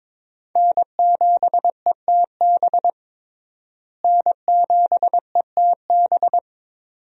Morse code, heard as a keyed tone, about 22 wpm, 700 Hz. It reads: N7ETB N7ETB
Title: USA Call Signs - 5 Characters - Repeated 2x - 22wpm